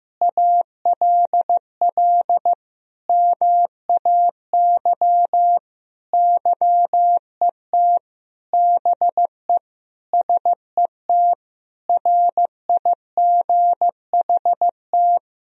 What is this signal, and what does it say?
Morse code, heard as a keyed tone, about 15 wpm, 700 Hz. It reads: ALL MAY YET BE SET RIGHT
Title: Sentences from Top 1000 Words - Review Part 1 - 15wpm